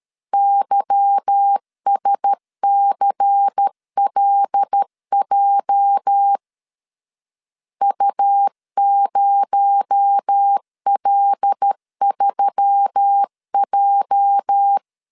Morse code, heard as a keyed tone, about 13 wpm, 800 Hz. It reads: YSCLJ U0L3J
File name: LEZIONI / 8 YJ45 / 8 YJ 12wpm 2235